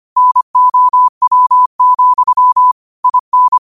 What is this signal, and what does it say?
Morse code, heard as a keyed tone, about 25 wpm, 1000 Hz. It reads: NOW, IN
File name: qotdfiles / 25wpm